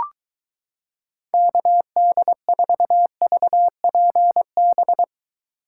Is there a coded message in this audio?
Yes: Morse code from 1.3 to 5.0 s, heard as a keyed tone, about 23 wpm, 700 Hz. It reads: KD4VPB